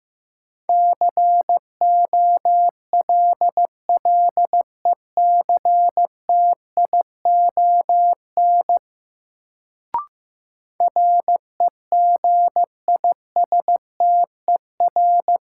Morse code, heard as a keyed tone, about 15 wpm, 700 Hz. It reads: COLLECTION REGISTER
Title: Sentences from Top 1200 Words - New Word Drill Alt - 15wpm